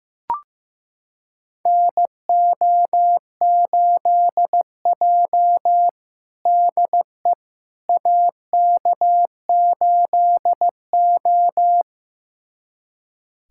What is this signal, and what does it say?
Morse code, heard as a keyed tone, about 15 wpm, 700 Hz. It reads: NO8J DE AK8O
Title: USA Callsign DE Callsign - 4 Characters - 15wpm